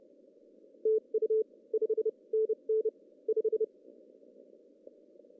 Morse code, heard as a keyed tone, about 30 wpm, spaced as for 25 wpm, 440 Hz. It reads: TU 5NN 5
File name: CW / cqww-cw-k5tr